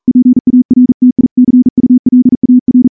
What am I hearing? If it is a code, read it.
Morse code, heard as a keyed tone, about 38 wpm, 250 Hz. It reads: PARTICULAR
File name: morse-training-40WPM